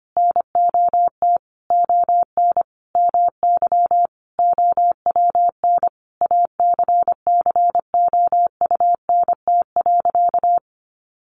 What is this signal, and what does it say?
Morse code, heard as a keyed tone, about 25 wpm, 700 Hz. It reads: NOT ON MY OWN ACCOUNT.
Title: Sentences from Top 1400 Words 25wpm